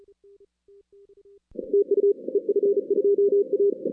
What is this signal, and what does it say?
Morse code, heard as a keyed tone, about 37 wpm, 410 Hz. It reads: TU EF2A
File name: 2016 arrl dx cw / 2016-ARRL-DX-CW-K5TR- - 0-46